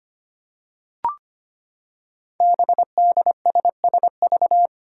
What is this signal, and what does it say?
Morse code, heard as a keyed tone, about 25 wpm, 700 Hz. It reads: BDSSV
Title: Five Challenging Letter-Number Combinations 25wpm